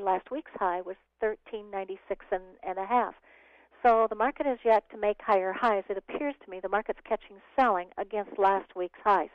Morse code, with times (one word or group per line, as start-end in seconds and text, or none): none